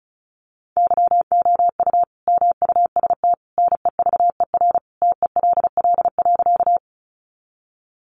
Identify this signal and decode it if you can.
Morse code, heard as a keyed tone, about 34 wpm, 700 Hz. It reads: YOU MUST NEVER TELL.
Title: Sentences from Top 300 Words 35wpm